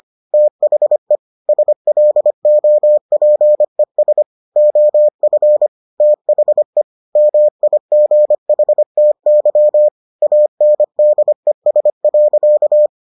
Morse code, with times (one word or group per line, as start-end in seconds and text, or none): 0.3-1.2 THE
1.5-4.2 SLOPES
4.6-5.7 OF
6.0-6.8 THE
7.2-9.9 MIGHTY
10.2-12.9 ANDES.